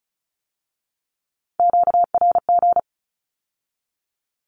Morse code, heard as a keyed tone, about 36 wpm, 700 Hz. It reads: QRG
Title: Q-Codes with Spoken Meaning - 35wpm